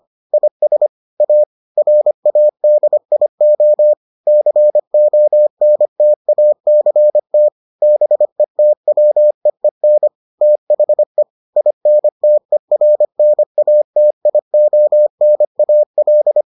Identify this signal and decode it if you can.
Morse code, heard as a keyed tone, about 25 wpm, 600 Hz. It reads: IS A RADIO CONTACT BETWEEN THE INTERNATIONAL